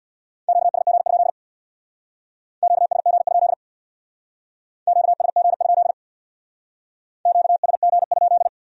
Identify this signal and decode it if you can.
Morse code, heard as a keyed tone, about 64 wpm, 700 Hz. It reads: QSG? QSG? QSG? QSG?